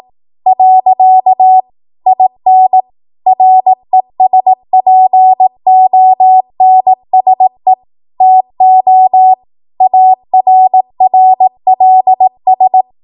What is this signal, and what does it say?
Morse code, heard as a keyed tone, about 18 wpm, 750 Hz. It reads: . IN RESPONSE TO ARRLS
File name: Morse / 18WPM